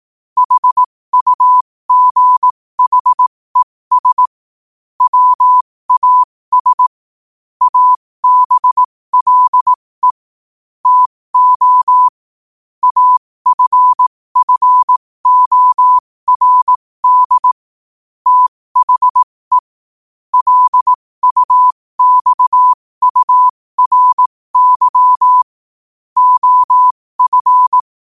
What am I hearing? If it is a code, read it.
Morse code, heard as a keyed tone, about 18 wpm, spaced as for 15 wpm, 1000 Hz. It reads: HUGHES WAS ABLE TO AFFORD THE LUXURY OF
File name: qotdfiles / 15wpm